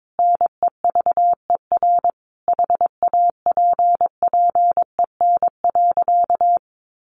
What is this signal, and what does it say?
Morse code, heard as a keyed tone, about 22 wpm, 700 Hz. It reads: NEVER HAPPEN.
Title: Sentences from Top 1200 Words 22wpm